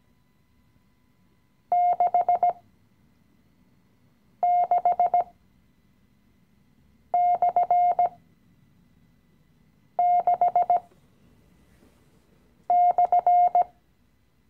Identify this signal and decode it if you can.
Morse code, heard as a keyed tone, about 17 wpm, spaced as for 4 wpm, 710 Hz. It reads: 66/6/
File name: CW-Online-Kurs / Anfänger / Lektion 016